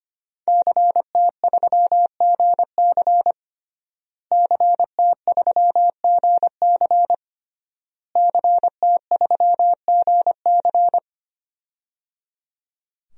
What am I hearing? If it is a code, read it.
Morse code, heard as a keyed tone, about 25 wpm, 700 Hz. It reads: CT3GC CT3GC CT3GC
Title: DXCC-100 Call Signs - 5 Characters - Repeated 3x - 25wpm